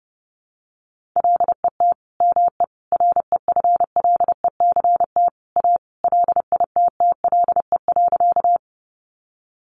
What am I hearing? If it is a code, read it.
Morse code, heard as a keyed tone, about 30 wpm, 700 Hz. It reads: LET ME REFLECT A LITTLE.
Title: Sentences from Top 1500 Words 30wpm